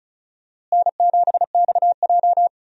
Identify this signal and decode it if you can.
Morse code, heard as a keyed tone, about 36 wpm, 700 Hz. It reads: N7XJ
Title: USA Call Signs - 4 Characters - Repeated 3x - 35wpm